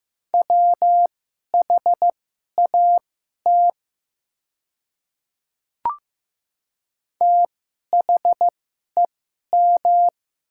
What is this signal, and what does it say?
Morse code, heard as a keyed tone, about 15 wpm, spaced as for 11 wpm, 700 Hz. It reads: WHAT THEM
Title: 2x Character Spacing - Top 100 Words - 15wpm